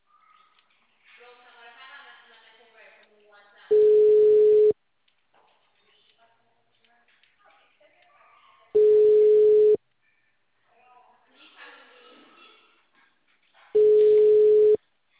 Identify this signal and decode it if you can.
Ringback tone (Europe, 425 Hz), with speech.